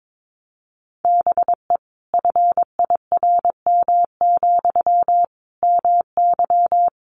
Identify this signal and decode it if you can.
Morse code, heard as a keyed tone, about 22 wpm, 700 Hz. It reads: BE FIRM, MY
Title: Sentences from Top 1400 Words 22wpm